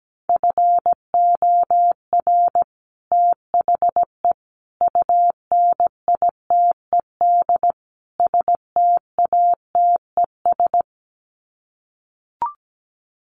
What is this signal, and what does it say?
Morse code, heard as a keyed tone, about 17 wpm, 700 Hz. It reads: FOR THE UNITED STATES
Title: Sets of 4 Words 17wpm